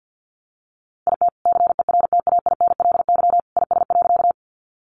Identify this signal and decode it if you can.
Morse code, heard as a keyed tone, about 50 wpm, 700 Hz. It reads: IT CERTAINLY IS.
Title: Sentences from Top 900 Words 50wpm